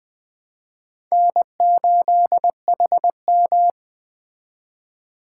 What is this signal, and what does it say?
Morse code, heard as a keyed tone, about 21 wpm, 700 Hz. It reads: N8HM
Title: Speed Racing - USA 1x2 Call Signs 17wpm